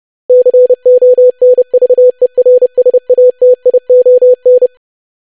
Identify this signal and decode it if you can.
Morse code, heard as a keyed tone, about 30 wpm, 500 Hz. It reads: CONVERSATION
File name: lesson / web / longwords-25wpm